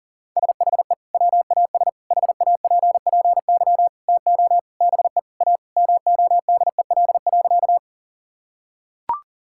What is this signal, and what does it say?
Morse code, heard as a keyed tone, about 40 wpm, 700 Hz. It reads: SHE WAS HAPPY TO BE A MODEL.
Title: Sentences from Top 1000 Words - Review Part 5 - 40wpm